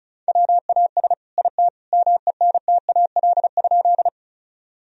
Morse code, heard as a keyed tone, about 34 wpm, 700 Hz. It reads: WAS IT MENTAL?
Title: Sentences from Top 1000 Words - Review Part 4 - 35wpm